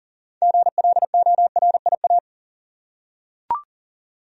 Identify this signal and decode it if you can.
Morse code, heard as a keyed tone, about 40 wpm, 700 Hz. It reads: GLORIA
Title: USA Common Names - 40wpm